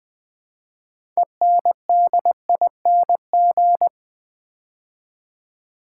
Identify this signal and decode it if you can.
Morse code, heard as a keyed tone, about 20 wpm, 700 Hz. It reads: ENDING
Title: Build A Word - 2 Letters At A Time - 6-Letter Words - 20wpm